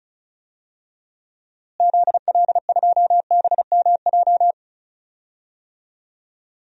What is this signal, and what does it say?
Morse code, heard as a keyed tone, about 34 wpm, 700 Hz. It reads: ZL2BMJ
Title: DXCC-100 Call Signs - 6 Characters - Repeated 3x - 35wpm